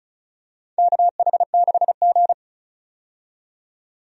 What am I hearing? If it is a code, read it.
Morse code, heard as a keyed tone, about 34 wpm, 700 Hz. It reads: KH6G
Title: DXCC-100 Call Signs - 4 Characters - 35wpm